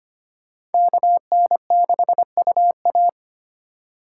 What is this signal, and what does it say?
Morse code, heard as a keyed tone, about 32 wpm, 700 Hz. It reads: KN6UA